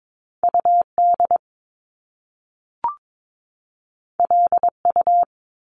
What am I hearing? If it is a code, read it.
Morse code, heard as a keyed tone, about 22 wpm, 700 Hz. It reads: UD LU
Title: Top Two Letter Combinations in Words 22wpm